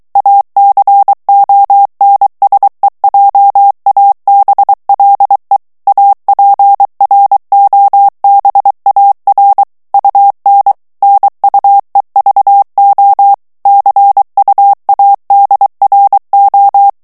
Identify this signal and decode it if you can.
Morse code, heard as a keyed tone, about 23 wpm, 800 Hz. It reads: ACONSEJABLE APROBAR UN NUEVO CUADRO
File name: morse mp3 / ESPAGNOL TEXTE EN CLAIR 2 VITESSE 22 MOTS MINUTES